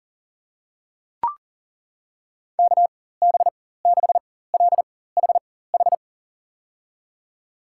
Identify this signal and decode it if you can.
Morse code, heard as a keyed tone, about 40 wpm, spaced as for 19 wpm, 700 Hz. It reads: KB6LHH
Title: USA Call Signs - 6 Characters - 4x Character Spacing - 40wpm